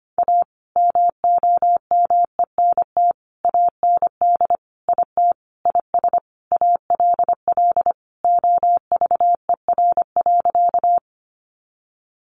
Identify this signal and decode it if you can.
Morse code, heard as a keyed tone, about 25 wpm, 700 Hz. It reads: A MOMENT AND IT IS ALL OVER.
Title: Sentences from Top 1000 Words 25wpm